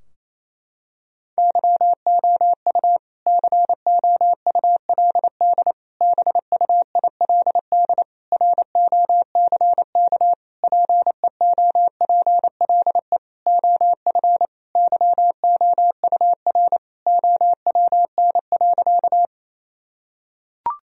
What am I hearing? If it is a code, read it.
Morse code, heard as a keyed tone, about 28 wpm, 700 Hz. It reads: YOU COULD BUILD ROCK PEOPLE OF YOUR OWN.
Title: Sentences from Top 1000 Words - Review Part 2 - 28wpm